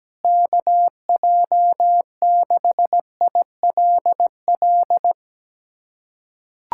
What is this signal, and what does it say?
Morse code, heard as a keyed tone, about 17 wpm, 700 Hz. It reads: KJ6ILL
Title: USA Callsign DE Callsign 17wpm